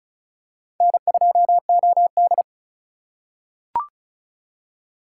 Morse code, heard as a keyed tone, about 36 wpm, 700 Hz. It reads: N2OD
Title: USA Call Signs - 4 Characters - 2x Character Spacing - 35wpm